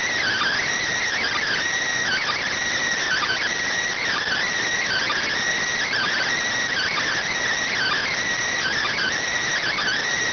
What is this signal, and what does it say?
Emergency Alert System (SAME) tones (header bursts), heard as harsh buzzing bursts. It is heard from 1.1 to 10.0 s.